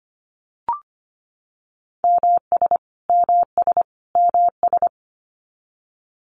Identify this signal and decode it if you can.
Morse code, heard as a keyed tone, about 25 wpm, 700 Hz. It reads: MS MS MS
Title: US States Repeated 3x 25wpm